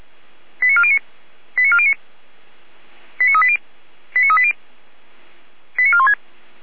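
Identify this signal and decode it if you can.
Five-tone selective call (ZVEI), heard as quick runs of five tones, tones of about 70 ms. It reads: 89409 89409 89380 89380 89416
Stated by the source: ZVEI five-tone call 89391 received by SDR on channel 455, fire brigade Orlamuende (feuerwehr-orlamuende).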